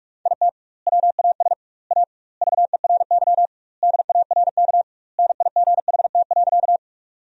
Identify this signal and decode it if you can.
Morse code, heard as a keyed tone, about 46 wpm, 700 Hz. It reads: IT WAS A VERY DARK NIGHT.
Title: 2x Word-Spacing - Sentences from Top 500 Words - 45wpm